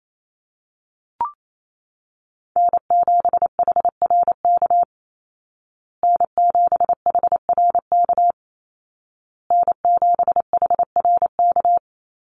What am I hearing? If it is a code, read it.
Morse code, heard as a keyed tone, about 28 wpm, 700 Hz. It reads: N7HRK N7HRK N7HRK
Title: Call Signs Repeated 2x - 28wpm